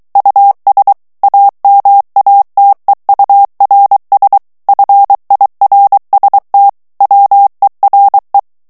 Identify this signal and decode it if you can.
Morse code, heard as a keyed tone, about 23 wpm, 800 Hz. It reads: US AMATEURS FIRST WERE
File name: ANGLAIS texte clair 3 vitesse 22 mots minute